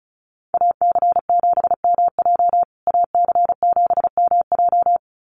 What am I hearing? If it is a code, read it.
Morse code, heard as a keyed tone, about 34 wpm, 700 Hz. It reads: AC7MJ AC7MJ
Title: USA Call Signs Repeated 2x 35wpm